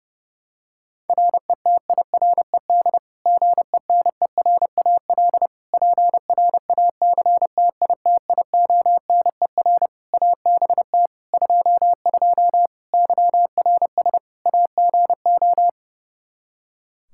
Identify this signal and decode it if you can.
Morse code, heard as a keyed tone, about 30 wpm, 700 Hz. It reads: RETIRED GENERAL PRACTITIONER ABT 22 YRS AGO
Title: Practice QSO - Exchange 3 - More Info - 30wpm